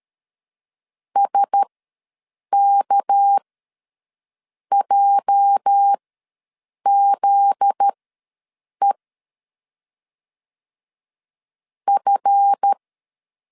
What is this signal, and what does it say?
Morse code, heard as a keyed tone, about 13 wpm, spaced as for 6 wpm, 800 Hz. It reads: SKJZE F
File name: LEZIONI / 8 YJ45 / 8 YJ ws 12wpm 3342